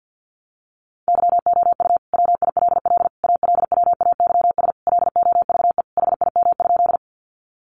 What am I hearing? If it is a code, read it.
Morse code, heard as a keyed tone, about 50 wpm, 700 Hz. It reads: YOU WILL ALWAYS LOVE HIM?
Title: Sentences from Top 400 Words 50wpm